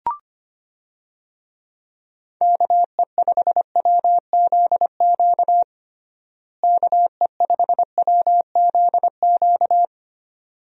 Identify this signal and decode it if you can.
Morse code, heard as a keyed tone, about 25 wpm, 700 Hz. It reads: KE5WZQ KE5WZQ
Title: USA Call Signs - 6 Characters - Repeated 2x - 25wpm